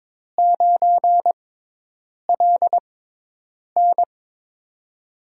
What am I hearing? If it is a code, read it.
Morse code, heard as a keyed tone, about 22 wpm, spaced as for 8 wpm, 700 Hz. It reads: 9LN
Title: USA Call Signs - 5 Characters - 6x Character Spacing - 22wpm